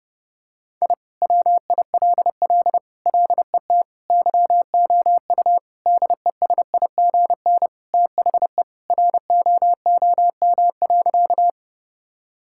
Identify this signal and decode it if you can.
Morse code, heard as a keyed tone, about 32 wpm, 700 Hz. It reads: I WILL LET YOU DESIGN THE ROOM.
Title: Sentences from Top 1000 Words - Review Part 5 - 30wpm